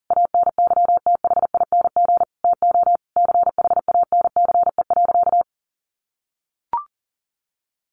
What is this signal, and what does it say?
Morse code, heard as a keyed tone, about 40 wpm, 700 Hz. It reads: ANYTHING TO CHANCE.
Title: Sentences from Top 1100 Words 40wpm